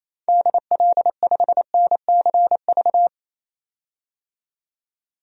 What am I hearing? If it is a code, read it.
Morse code, heard as a keyed tone, about 28 wpm, 700 Hz. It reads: DL5NCV
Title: DXCC-100 Call Signs - 6 Characters - 28wpm